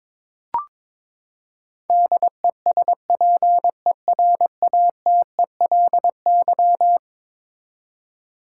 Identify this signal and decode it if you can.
Morse code, heard as a keyed tone, about 22 wpm, 700 Hz. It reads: DESPERATELY
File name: practice / 22wpm / Eleven Letter Words - 22wpm